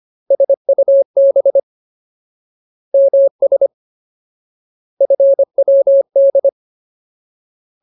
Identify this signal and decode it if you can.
Morse code, heard as a keyed tone, about 25 wpm, 550 Hz. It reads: SUB MS FWD
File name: Ultimate Word List / 25WPM Ultimate